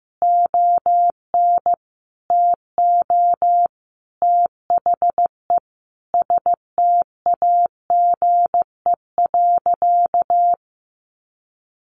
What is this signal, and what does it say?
Morse code, heard as a keyed tone, about 15 wpm, 700 Hz. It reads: ON TO THE STAGE.
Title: Sentences from Top 1500 Words 15wpm